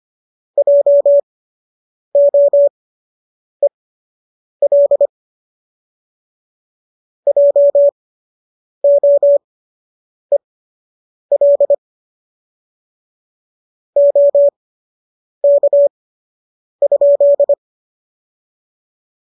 Morse code, heard as a keyed tone, about 25 wpm, spaced as for 8 wpm, 570 Hz. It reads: JOEL JOEL OK?